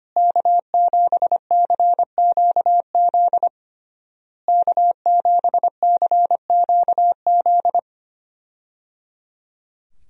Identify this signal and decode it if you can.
Morse code, heard as a keyed tone, about 25 wpm, 700 Hz. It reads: K7CQZ K7CQZ
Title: USA Call Signs - 5 Characters - Repeated 3x - 25wpm